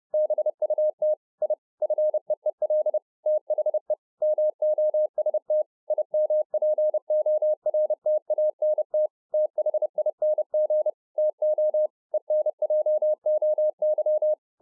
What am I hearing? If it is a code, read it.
Morse code, heard as a keyed tone, about 30 wpm, 610 Hz. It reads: BUT I FEEL THE MOST IMPORTANT THING TO ENJOY